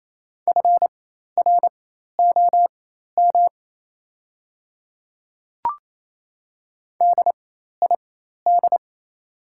Morse code, heard as a keyed tone, about 28 wpm, spaced as for 13 wpm, 700 Hz. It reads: FROM DID
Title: 4x Character Spacing - Top 100 Words - 28wpm